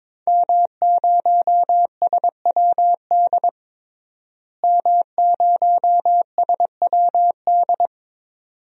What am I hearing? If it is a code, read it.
Morse code, heard as a keyed tone, about 22 wpm, 700 Hz. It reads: M0SWD M0SWD
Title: DXCC-100 Call Signs - 5 Characters - Repeated 2x - 22wpm